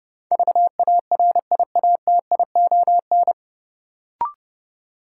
Morse code, heard as a keyed tone, about 30 wpm, 700 Hz. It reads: VARIATION
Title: Sentences from Top 1600 Words - New Word Drill Alt - 30wpm